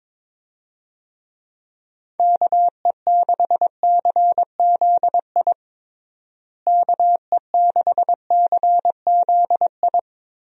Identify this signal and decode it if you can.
Morse code, heard as a keyed tone, about 22 wpm, 700 Hz. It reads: KE6CZI KE6CZI
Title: USA Call Signs - 6 Characters - Repeated 2x - 22wpm